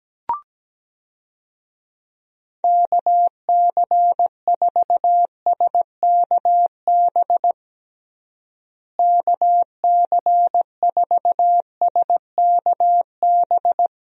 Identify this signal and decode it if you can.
Morse code, heard as a keyed tone, about 17 wpm, 700 Hz. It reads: KC4SKB KC4SKB
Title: USA Call Signs - 6 Characters - Repeated 2x - 17wpm